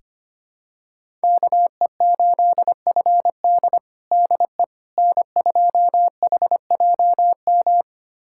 Morse code, heard as a keyed tone, about 25 wpm, 700 Hz. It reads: KE8FD DE N2HJM
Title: USA Callsign DE Callsign - 5 Characters - 25wpm